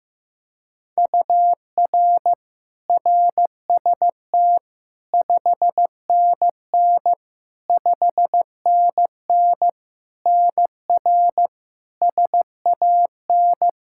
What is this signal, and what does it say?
Morse code, heard as a keyed tone, about 15 wpm, 700 Hz. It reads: UR RST 5NN 5NN NR SAN
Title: Practice QSO - Exchange 1 - Beginning - 15wpm